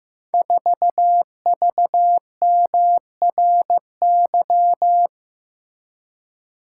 Morse code, heard as a keyed tone, about 15 wpm, 700 Hz. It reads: 4VMRY
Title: Code Groups - 5 Letter:Numbers - 15wpm